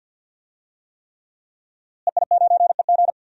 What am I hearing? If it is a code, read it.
Morse code, heard as a keyed tone, about 51 wpm, 700 Hz. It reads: EI9EG
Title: DXCC-100 Call Signs - 5 Characters - Repeated 3x - 50wpm